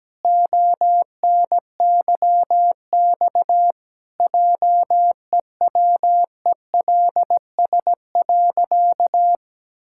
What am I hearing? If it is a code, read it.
Morse code, heard as a keyed tone, about 17 wpm, 700 Hz. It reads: ONYX JEWELS.
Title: Pangram Challenge v2 - 17wpm